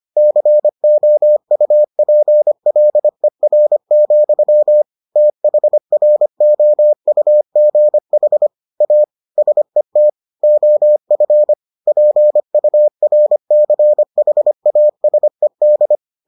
Morse code, heard as a keyed tone, about 25 wpm, 600 Hz. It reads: COUPLER, THROUGH A SET OF PURCHASED